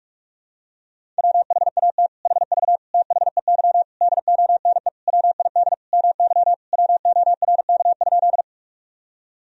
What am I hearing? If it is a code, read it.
Morse code, heard as a keyed tone, about 42 wpm, 700 Hz. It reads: WHAT HV THEY DONE WID MY WORK?
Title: Sentences from Top 500 Words and Abbreviations 45wpm